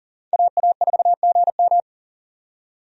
Morse code, heard as a keyed tone, about 37 wpm, 700 Hz. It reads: AA4GM
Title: USA Call Signs - 5 Characters - 40wpm